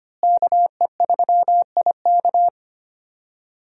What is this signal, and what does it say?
Morse code, heard as a keyed tone, about 25 wpm, 700 Hz. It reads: KE3IK
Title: USA Call Signs - 5 Characters - Repeated 3x - 25wpm